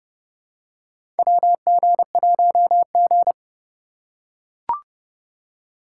Morse code, heard as a keyed tone, about 32 wpm, 700 Hz. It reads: WG1G
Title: USA Call Signs - 2x Character Spacing - 30wpm